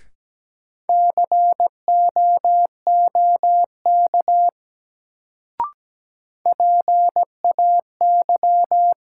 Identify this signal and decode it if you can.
Morse code, heard as a keyed tone, about 17 wpm, 700 Hz. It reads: COOK PAY
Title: Sentences from Top 700 Words - New Word Drill - 17wpm